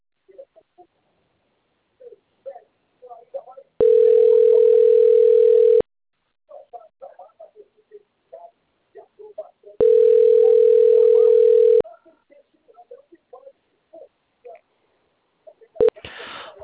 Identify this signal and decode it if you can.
Ringback tone (North America), 440 + 480 Hz, with speech.
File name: rg-1600-unknown- - - .35890